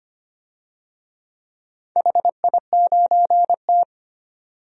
Morse code, heard as a keyed tone, about 25 wpm, 700 Hz. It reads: HI9T
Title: DXCC-100 Call Signs - 4 Characters - 25wpm